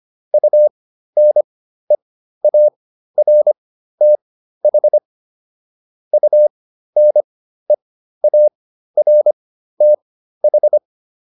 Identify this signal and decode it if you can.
Morse code, heard as a keyed tone, about 25 wpm, spaced as for 13 wpm, 600 Hz. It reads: UNEARTH UNEARTH